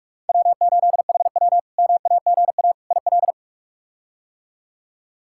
Morse code, heard as a keyed tone, about 43 wpm, 700 Hz. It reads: W8HW MAGA IL
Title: K1USN SST QSO Practice - 45wpm